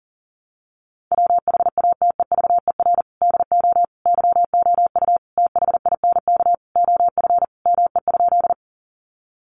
Morse code, heard as a keyed tone, about 40 wpm, 700 Hz. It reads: WHATEVER DO YOU THINK OF ME?
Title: Sentences from Top 1100 Words 40wpm